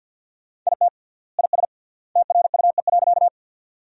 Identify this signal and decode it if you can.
Morse code, heard as a keyed tone, about 53 wpm, 700 Hz. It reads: IT IS TRUE.
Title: 3x Word-Spacing - Sentences from Top 400 Words - 50wpm